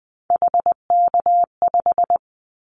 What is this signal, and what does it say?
Morse code, heard as a keyed tone, about 20 wpm, 700 Hz. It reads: HK5
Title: Callsign Prefixes for Top 100 DXCC - 20wpm